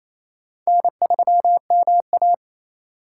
Morse code, heard as a keyed tone, about 39 wpm, spaced as for 34 wpm, 700 Hz. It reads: N3MA
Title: Speed Racing - USA 1x2 Call Signs 28wpm